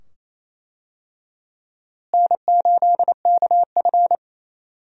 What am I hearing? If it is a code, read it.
Morse code, heard as a keyed tone, about 28 wpm, 700 Hz. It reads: N8KF
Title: USA Call Signs - 4 Characters - Repeated 2x - 28wpm